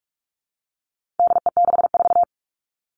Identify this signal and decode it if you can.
Morse code, heard as a keyed tone, about 44 wpm, 700 Hz. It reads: DE64
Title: All Cut Numbers - 4-Digit 45wpm